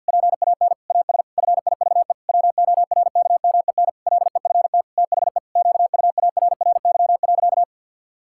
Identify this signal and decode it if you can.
Morse code, heard as a keyed tone, about 51 wpm, 700 Hz. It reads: PAN AS FIVE WORKMEN LEFT THE QUARRY.
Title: Pangram Challenge v2 - 50wpm